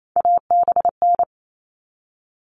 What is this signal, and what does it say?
Morse code, heard as a keyed tone, about 28 wpm, 700 Hz. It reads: ABN
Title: All Cut Numbers - 3-Digit 28wpm